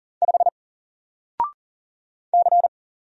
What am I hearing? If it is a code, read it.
Morse code, heard as a keyed tone, about 40 wpm, 700 Hz. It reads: 5 C
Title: Single Letter-Number Alt 40wpm